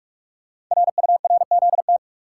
Morse code, heard as a keyed tone, about 44 wpm, 690 Hz. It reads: RURZT
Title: Code Groups - 5 Letters - 45wpm